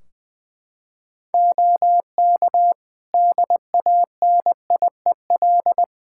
Morse code, heard as a keyed tone, about 20 wpm, 700 Hz. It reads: OK DANIEL
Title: Practice QSO - Exchange 4 - Ending - 20wpm